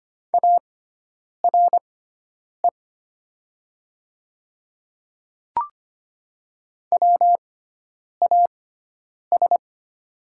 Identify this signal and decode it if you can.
Morse code, heard as a keyed tone, about 25 wpm, spaced as for 9 wpm, 690 Hz. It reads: ARE WAS
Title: 6x Character Spacing - Top 100 Words - 25wpm